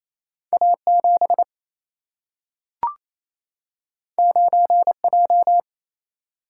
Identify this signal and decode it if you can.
Morse code, heard as a keyed tone, about 28 wpm, 700 Hz. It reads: A7 9J
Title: Single-Letter Single-Number 28wpm